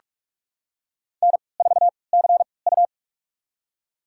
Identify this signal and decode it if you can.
Morse code, heard as a keyed tone, about 47 wpm, spaced as for 26 wpm, 700 Hz. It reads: N4CU
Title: USA Call Signs - 4 Characters - 3x Character Spacing - 45wpm